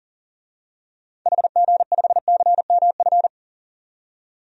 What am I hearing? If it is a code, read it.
Morse code, heard as a keyed tone, about 40 wpm, 700 Hz. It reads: HG5CMF
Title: DXCC-100 Call Signs - 6 Characters - 40wpm